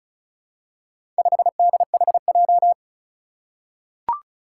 Morse code, heard as a keyed tone, about 36 wpm, 700 Hz. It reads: 5DHJ
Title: Four Challenging Letter-Number Combinations 35wpm